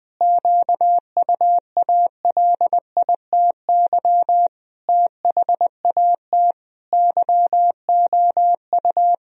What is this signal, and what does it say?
Morse code, heard as a keyed tone, about 20 wpm, 700 Hz. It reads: QUALITY THAT YOU
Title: Sentences from Top 1000 Words - Review Part 4 - 20wpm